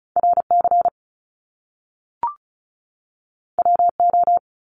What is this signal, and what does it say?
Morse code, heard as a keyed tone, about 36 wpm, 700 Hz. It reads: RC WO
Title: Top Two Letter Combinations in Words 35wpm